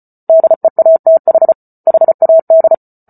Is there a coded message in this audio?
Yes: Morse code from 0.3 to 2.7 s, heard as a keyed tone, about 34 wpm, 650 Hz. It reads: DEATH HAD